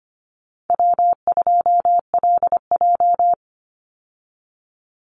Morse code, heard as a keyed tone, about 25 wpm, 700 Hz. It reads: W2LJ
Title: Easy Call Signs Repeated 2x - 25wpm